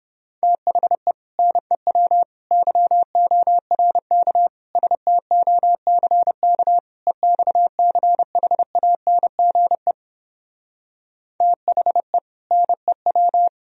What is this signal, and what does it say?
Morse code, heard as a keyed tone, about 31 wpm, 700 Hz. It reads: THE NEW YORK STOCK EXCHANGE THE NEW